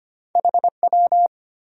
Morse code, heard as a keyed tone, about 25 wpm, 700 Hz. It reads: HW
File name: Top 90 QSO Elements - 3x Character Spacing - 25wpm